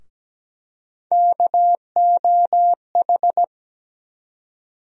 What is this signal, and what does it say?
Morse code, heard as a keyed tone, about 17 wpm, 700 Hz. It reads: KOH
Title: USA Call Sign Postfixes - 3 Letters 17wpm